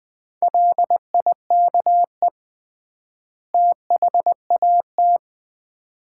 Morse code, heard as a keyed tone, about 20 wpm, 700 Hz. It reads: LIKE THAT
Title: 3x Word-Spacing - Sets of 2 Words - 20wpm